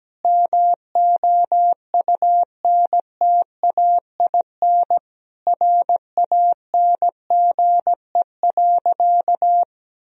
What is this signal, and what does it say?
Morse code, heard as a keyed tone, about 17 wpm, 700 Hz. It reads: MOUNTAIN RANGE.
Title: Sentences from Top 1000 Words - Review Part 4 - 17wpm